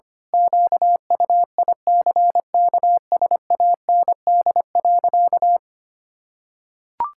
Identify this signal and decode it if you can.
Morse code, heard as a keyed tone, about 25 wpm, 700 Hz. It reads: QUICKSAND.
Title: Pangram Challenge v2 - 25wpm